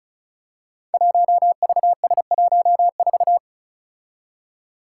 Morse code, heard as a keyed tone, about 36 wpm, 700 Hz. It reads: 1VS14
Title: Five Challenging Letter-Number Combinations 35wpm